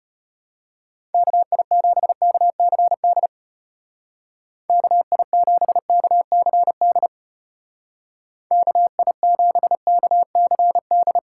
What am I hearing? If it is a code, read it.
Morse code, heard as a keyed tone, about 31 wpm, 700 Hz. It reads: KI7KCD KI7KCD KI7KCD
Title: Speed Racing - USA 2x3 Call Signs 25wpm